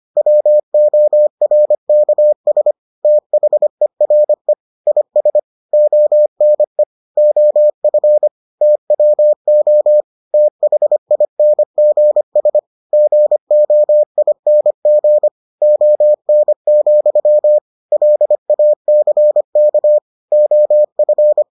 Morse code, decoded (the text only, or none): WORKS THERE IS ONE OF TWO THINGS GOING ON, LACK OF